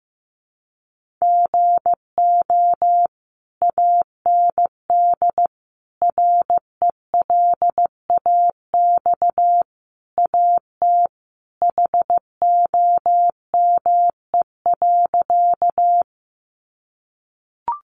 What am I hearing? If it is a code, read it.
Morse code, heard as a keyed tone, about 15 wpm, 700 Hz. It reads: GO AND RELAX AT HOME.
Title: Sentences from Top 1400 Words 15wpm